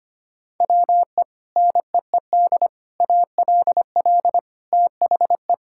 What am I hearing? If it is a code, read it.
Morse code, heard as a keyed tone, about 25 wpm, 700 Hz. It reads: WE NEED ALL THE
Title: Sentences from Top 1000 Words - Review Part 5 - 25wpm